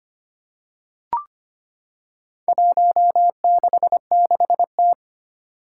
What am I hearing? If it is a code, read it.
Morse code, heard as a keyed tone, about 25 wpm, 700 Hz. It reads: 166T
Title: Four Challenging Letter-Number Combinations 25wpm